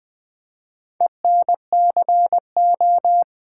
Morse code, heard as a keyed tone, about 20 wpm, 700 Hz. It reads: ENCO
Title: Build A Word - 4 Letters At A Time - 11-Letter Words - 20wpm